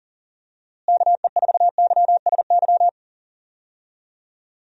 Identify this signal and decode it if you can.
Morse code, heard as a keyed tone, about 40 wpm, 700 Hz. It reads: KE4YSY
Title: USA Call Signs - 6 Characters - Repeated 2x - 40wpm